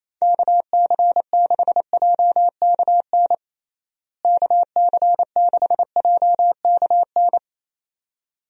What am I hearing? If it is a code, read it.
Morse code, heard as a keyed tone, about 28 wpm, 700 Hz. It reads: KC6JKN KC6JKN
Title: USA Call Signs - 6 Characters - Repeated 2x - 28wpm